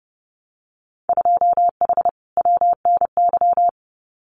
Morse code, heard as a keyed tone, about 28 wpm, 700 Hz. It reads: 2H WNY
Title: Winter Field Day Exchanges Encore 30wpm